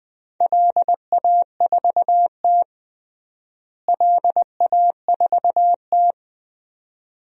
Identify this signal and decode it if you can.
Morse code, heard as a keyed tone, about 20 wpm, 700 Hz. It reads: LA4T LA4T
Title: DXCC-100 Call Signs - 4 Characters - Repeated 2x - 20wpm